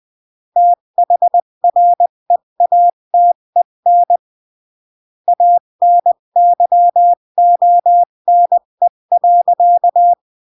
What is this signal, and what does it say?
Morse code, heard as a keyed tone, about 20 wpm, 700 Hz. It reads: THREATEN ANYONE.
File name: CW / jodorowsky / 82 commandments 20-15